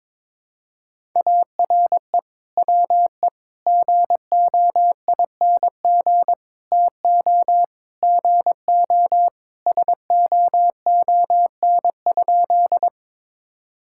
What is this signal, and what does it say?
Morse code, heard as a keyed tone, about 22 wpm, 700 Hz. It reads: ARE WE GOING TO GO SOON?
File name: practice / 22wpm / Sentences with 3-Word Phrases - 22wpm